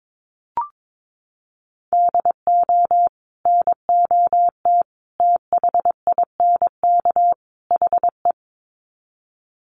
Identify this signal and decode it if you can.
Morse code, heard as a keyed tone, about 22 wpm, 700 Hz. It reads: DO NOT THINK HE
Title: Sets of 4 Words 22wpm